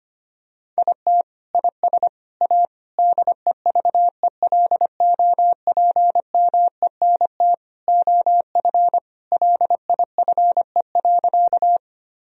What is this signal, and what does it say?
Morse code, heard as a keyed tone, about 25 wpm, 700 Hz. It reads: IT IS A DEVELOPMENT OF LIFE.
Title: Sentences from Top 1000 Words - Review Part 4 - 25wpm